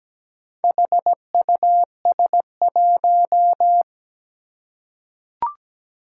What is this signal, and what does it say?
Morse code, heard as a keyed tone, about 17 wpm, 700 Hz. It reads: HUS1
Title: Four Challenging Letter-Number Combinations 17wpm